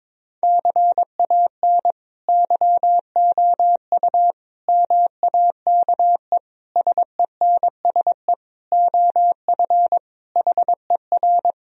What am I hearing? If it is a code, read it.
Morse code, heard as a keyed tone, about 21 wpm, 700 Hz. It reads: CAN YOU MAKE SENSE OF HER
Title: Sentences from Top 1000 Words - Review Part 4 - 22wpm